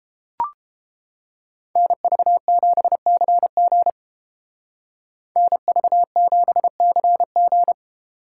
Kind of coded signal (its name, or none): Morse code